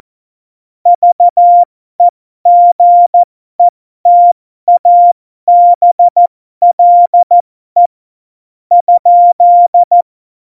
Morse code, heard as a keyed tone, about 14 wpm, spaced as for 11 wpm, 700 Hz. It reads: VEGETABLE?
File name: Morsekurs / Witze